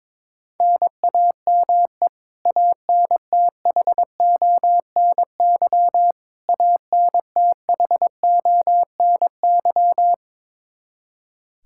Morse code, heard as a keyed tone, about 21 wpm, 700 Hz. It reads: NAME ANTHONY ANTHONY
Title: Practice QSO - Exchange 1 - Beginning - 22wpm